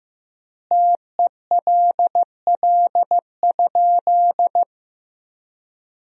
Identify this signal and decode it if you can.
Morse code, heard as a keyed tone, about 15 wpm, 700 Hz. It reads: TELL?
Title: 3x Word-Spacing - Sentences from Top 300 Words - 15wpm